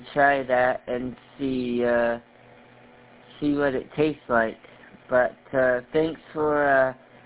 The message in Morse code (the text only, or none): none